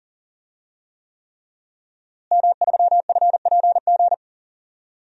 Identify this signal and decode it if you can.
Morse code, heard as a keyed tone, about 40 wpm, 700 Hz. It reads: M3FPG
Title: DXCC-100 Call Signs - 5 Characters - Repeated 3x - 40wpm